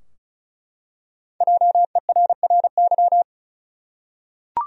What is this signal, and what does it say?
Morse code, heard as a keyed tone, about 34 wpm, 700 Hz. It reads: JERRY
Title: USA Common Names - Max 5 Letters - 35wpm